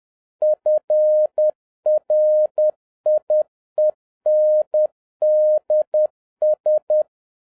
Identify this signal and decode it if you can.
Morse code, heard as a keyed tone, about 10 wpm, 610 Hz. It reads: FRIENDS